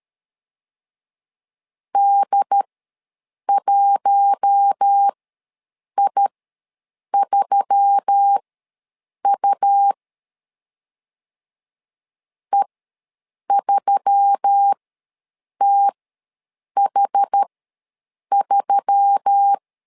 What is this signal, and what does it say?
Morse code, heard as a keyed tone, about 13 wpm, spaced as for 7 wpm, 800 Hz. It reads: D1I3U E3TH3
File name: LEZIONI / 4 Q1S3 ws 12wpm 2038